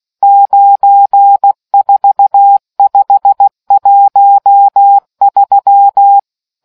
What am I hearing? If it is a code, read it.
Morse code, heard as a keyed tone, about 16 wpm, 800 Hz. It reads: 94513